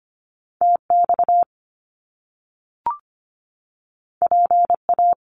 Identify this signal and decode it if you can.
Morse code, heard as a keyed tone, about 25 wpm, 700 Hz. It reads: TX PA
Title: US States 25wpm